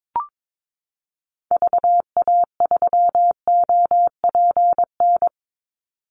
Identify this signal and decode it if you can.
Morse code, heard as a keyed tone, about 22 wpm, 700 Hz. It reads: VA3OPN
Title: Call Signs Repeated 3x - 22wpm